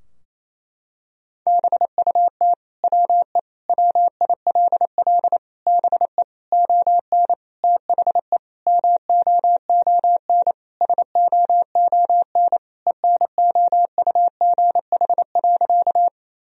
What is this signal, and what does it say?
Morse code, heard as a keyed tone, about 28 wpm, 700 Hz. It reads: BUT WE WILL BE ON THE MOON SOON ENOUGH.
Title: Sentences from Top 1000 Words - Review Part 3 - 28wpm